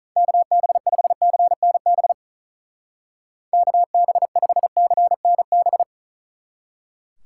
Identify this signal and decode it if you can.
Morse code, heard as a keyed tone, about 44 wpm, 700 Hz. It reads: KB5CNB KB5CNB